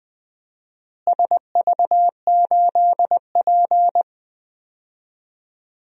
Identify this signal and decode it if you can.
Morse code, heard as a keyed tone, about 20 wpm, 700 Hz. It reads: SV8P
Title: DXCC-100 Call Signs - 4 Characters - Repeated 2x - 20wpm